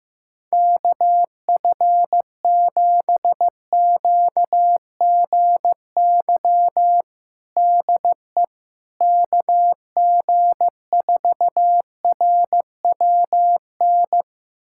Morse code, heard as a keyed tone, about 15 wpm, 700 Hz. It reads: KF7QGY DE KG4RWN
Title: USA Callsign DE Callsign - 6 Characters - 15wpm